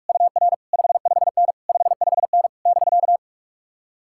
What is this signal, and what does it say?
Morse code, heard as a keyed tone, about 54 wpm, 700 Hz. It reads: UR 55N 55N <BK>
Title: Speed-Racing Common QSO Phrases - 45wpm